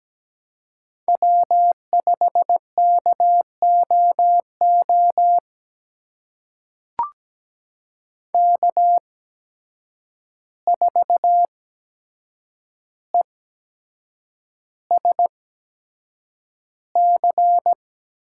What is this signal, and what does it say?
Morse code, heard as a keyed tone, about 17 wpm, spaced as for 7 wpm, 700 Hz. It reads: W5KOO K4ESC
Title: USA Call Signs - 5 Characters - 8x Character Spacing - 17wpm